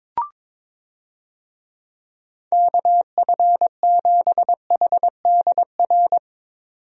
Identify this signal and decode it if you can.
Morse code, heard as a keyed tone, about 22 wpm, 700 Hz. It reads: KF7HDR
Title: USA Call Signs - 6 Characters - 22wpm